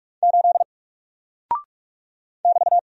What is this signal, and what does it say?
Morse code, heard as a keyed tone, about 44 wpm, 700 Hz. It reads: 8 <BT>
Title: Single Character (Letter:Number:Punctuation:Prosign) Alt 45wpm